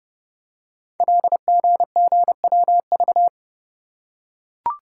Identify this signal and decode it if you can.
Morse code, heard as a keyed tone, about 30 wpm, 700 Hz. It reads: LGGWV
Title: Code Groups - 5 Letters - 30wpm